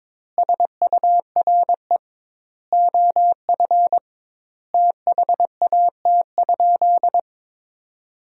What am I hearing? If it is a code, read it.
Morse code, heard as a keyed tone, about 22 wpm, 700 Hz. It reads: SURE OF THAT?
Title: 2x Word-Spacing - Sentences from Top 300 Words - 22wpm